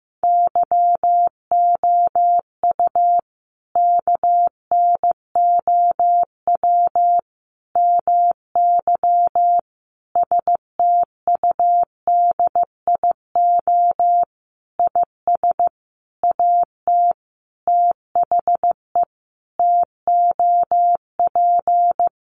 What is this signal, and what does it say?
Morse code, heard as a keyed tone, about 15 wpm, 700 Hz. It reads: YOU KNOW MY STUDIO IS AT THE TOP
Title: Sentences from Top 1200 Words 15wpm